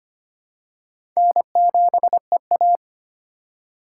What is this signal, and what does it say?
Morse code, heard as a keyed tone, about 25 wpm, 700 Hz. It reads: N7EA
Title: USA Call Signs - 4 Characters - 2x Character Spacing - 25wpm